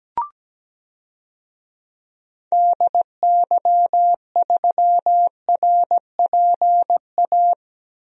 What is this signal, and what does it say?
Morse code, heard as a keyed tone, about 17 wpm, 700 Hz. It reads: DY3RPA
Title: DXCC-100 Call Signs - 6 Characters - Repeated 2x - 17wpm